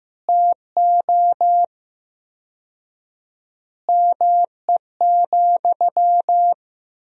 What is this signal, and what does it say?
Morse code, heard as a keyed tone, about 15 wpm, 700 Hz. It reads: TO ME,
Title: 4x Word-Spacing - Sentences from Top 500 Words - 15wpm